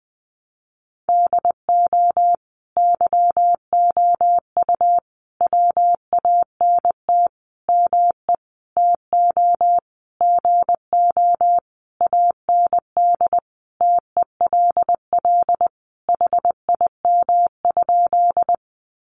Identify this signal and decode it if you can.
Morse code, heard as a keyed tone, about 20 wpm, 700 Hz. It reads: DO YOU WANT ME TO GO AND TELL HIM?
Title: Sentences from Top 200 Words 20wpm